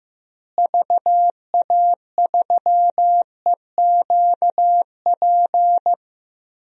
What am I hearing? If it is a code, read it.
Morse code, heard as a keyed tone, about 15 wpm, 700 Hz. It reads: VA3EQP
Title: DXCC-100 Call Signs - 6 Characters - Repeated 3x - 15wpm